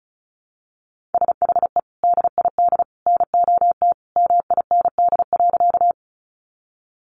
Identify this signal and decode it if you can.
Morse code, heard as a keyed tone, about 34 wpm, 700 Hz. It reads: SHE DID NOT MIND.
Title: Sentences from Top 600 Words 35wpm